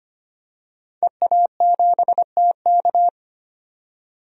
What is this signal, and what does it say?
Morse code, heard as a keyed tone, about 25 wpm, 700 Hz. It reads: EA7TK
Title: DXCC-100 Call Signs - 5 Characters - 25wpm